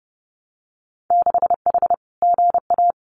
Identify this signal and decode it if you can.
Morse code, heard as a keyed tone, about 30 wpm, 700 Hz. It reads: 6H GA
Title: Winter Field Day Exchanges Encore 30wpm